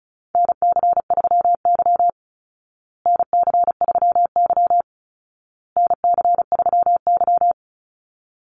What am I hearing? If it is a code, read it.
Morse code, heard as a keyed tone, about 34 wpm, 700 Hz. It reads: NC3Y NC3Y NC3Y
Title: Call Signs Repeated 2x - 35wpm